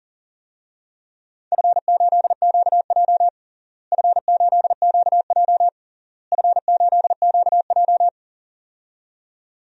Morse code, heard as a keyed tone, about 37 wpm, 700 Hz. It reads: F8QJ F8QJ F8QJ
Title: DXCC-100 Call Signs - 4 Characters - Repeated 3x - 40wpm